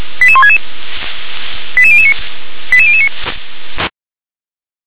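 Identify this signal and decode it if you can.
Five-tone selective call (ZVEI), heard as quick runs of five tones, tones of about 70 ms. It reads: 88166 88009 88009
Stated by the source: ZVEI five-tone selective call, address 88133, received on an SDR.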